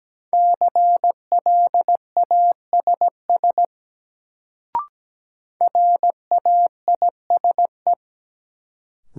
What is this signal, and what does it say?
Morse code, heard as a keyed tone, about 17 wpm, 700 Hz. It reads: CLASS RAISE
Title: Sentences from Top 1400 Words - New Word Drill Alt - 17wpm